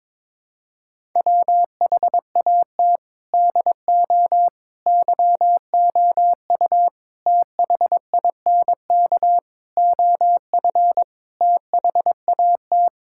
Morse code, heard as a keyed tone, about 22 wpm, 700 Hz. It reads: WHAT DO YOU THINK OF THAT
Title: Sentences from Top 1000 Words - Review Part 6 - 22wpm